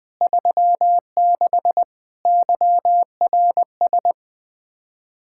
Morse code, heard as a keyed tone, about 20 wpm, 700 Hz. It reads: 36 YRS
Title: Practice QSO - Exchange 3 - More Info - 20wpm